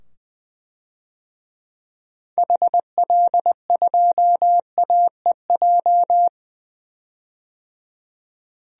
Morse code, heard as a keyed tone, about 20 wpm, 700 Hz. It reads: HL2AEJ
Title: DXCC-100 Call Signs - 6 Characters - 20wpm